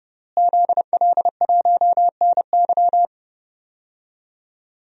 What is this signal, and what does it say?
Morse code, heard as a keyed tone, about 30 wpm, 700 Hz. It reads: ZL1NY
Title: DXCC-100 Call Signs - 5 Characters - Repeated 3x - 30wpm